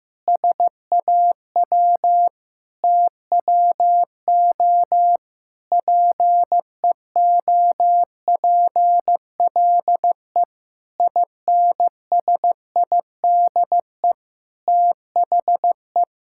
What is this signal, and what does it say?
Morse code, heard as a keyed tone, about 15 wpm, 700 Hz. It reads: SAW TWO PEOPLE INSIDE THE
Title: Sentences from Top 1000 Words - Review Part 6 - 15wpm